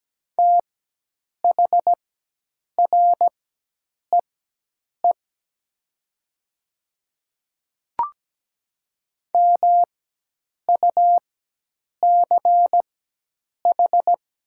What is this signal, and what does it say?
Morse code, heard as a keyed tone, about 17 wpm, spaced as for 8 wpm, 700 Hz. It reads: THREE MUCH
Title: 4x Character Spacing - Top 300 Words - 17wpm